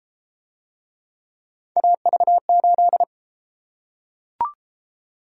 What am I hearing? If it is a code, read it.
Morse code, heard as a keyed tone, about 29 wpm, 700 Hz. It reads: AV8